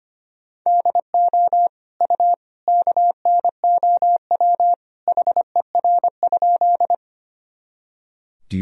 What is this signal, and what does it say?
Morse code, heard as a keyed tone, about 25 wpm, 700 Hz. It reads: DO U KNOW HER?
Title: Sentences from Top 100 Words and Abbreviations 25wpm